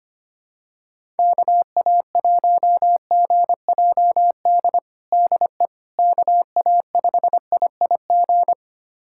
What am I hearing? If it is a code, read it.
Morse code, heard as a keyed tone, about 25 wpm, 700 Hz. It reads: KA1GJD DE KA5IIG
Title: USA Callsign DE Callsign - 6 Characters - 25wpm